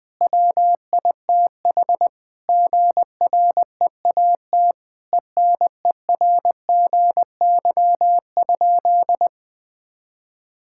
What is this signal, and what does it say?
Morse code, heard as a keyed tone, about 20 wpm, 700 Hz. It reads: WITH GREAT ENERGY?
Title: Sentences from Top 1000 Words - Review Part 6 - 20wpm